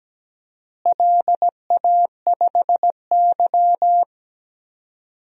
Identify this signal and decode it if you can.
Morse code, heard as a keyed tone, about 17 wpm, 700 Hz. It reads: LA5Y
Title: DXCC-100 Call Signs - 4 Characters - Repeated 2x - 17wpm